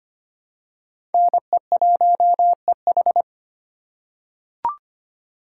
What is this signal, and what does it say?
Morse code, heard as a keyed tone, about 25 wpm, 700 Hz. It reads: NE1EH
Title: USA Call Signs - 3x Character Spacing - 25wpm